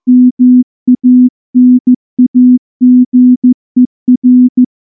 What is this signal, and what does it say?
Morse code, heard as a keyed tone, about 14 wpm, 250 Hz. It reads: MANAGER
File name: morse / morse-training-15WPM